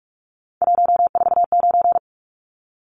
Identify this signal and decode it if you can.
Morse code, heard as a keyed tone, about 44 wpm, 700 Hz. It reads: 149
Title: Numbers - 3-Digit 45wpm